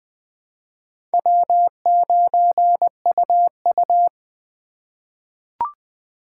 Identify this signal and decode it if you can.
Morse code, heard as a keyed tone, about 20 wpm, 700 Hz. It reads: W9UU
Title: USA Call Signs - 4 Characters - 2x Character Spacing - 20wpm